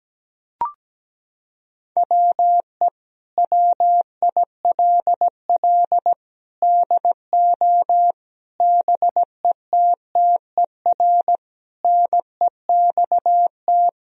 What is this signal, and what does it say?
Morse code, heard as a keyed tone, about 17 wpm, 700 Hz. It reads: WE WILL DO BETTER NEXT
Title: Sentences from Top 1000 Words - Review Part 1 - 17wpm